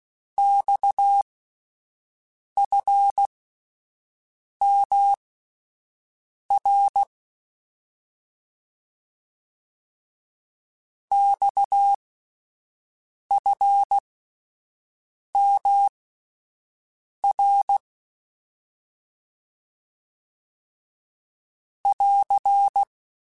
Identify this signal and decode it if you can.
Morse code, heard as a keyed tone, about 16 wpm, spaced as for 5 wpm, 800 Hz. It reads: XFMR XFMR <AR>
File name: Abreviations / Abreviations mp3s / Abreviations@5